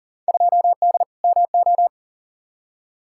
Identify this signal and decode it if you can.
Morse code, heard as a keyed tone, about 40 wpm, 700 Hz. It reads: 2D MO
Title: Field Day Exchanges - 40wpm